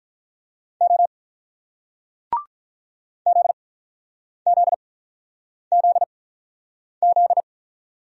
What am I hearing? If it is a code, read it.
Morse code, heard as a keyed tone, about 48 wpm, spaced as for 9 wpm, 700 Hz. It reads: O ZZZZ